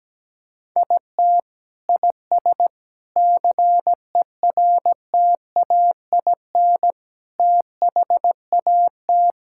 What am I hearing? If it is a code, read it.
Morse code, heard as a keyed tone, about 17 wpm, 700 Hz. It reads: IT IS CERTAIN THAT
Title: Sentences from Top 1000 Words - Review Part 5 - 17wpm